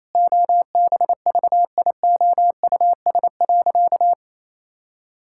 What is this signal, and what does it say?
Morse code, heard as a keyed tone, about 28 wpm, 700 Hz. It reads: OBVIOUS.
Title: Sentences from Top 1600 Words 28wpm